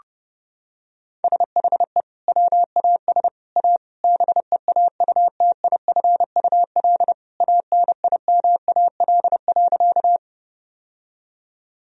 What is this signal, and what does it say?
Morse code, heard as a keyed tone, about 32 wpm, 700 Hz. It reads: SHE WAS A BEAUTIFUL ANIMAL.
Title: Sentences from Top 1000 Words - Review Part 1 - 30wpm